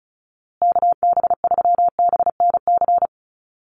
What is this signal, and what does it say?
Morse code, heard as a keyed tone, about 34 wpm, 700 Hz. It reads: KB3BNC
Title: USA Call Signs Repeated 3x 35wpm